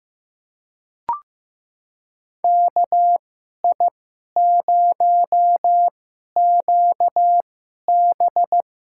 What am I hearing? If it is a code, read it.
Morse code, heard as a keyed tone, about 15 wpm, spaced as for 11 wpm, 700 Hz. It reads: KI0QB
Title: USA Call Signs - 5 Characters - 2x Character Spacing - 15wpm